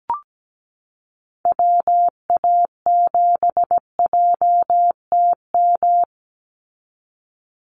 Morse code, heard as a keyed tone, about 17 wpm, 700 Hz. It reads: WA7JTM
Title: NA SOTA Activator Callsigns 17wpm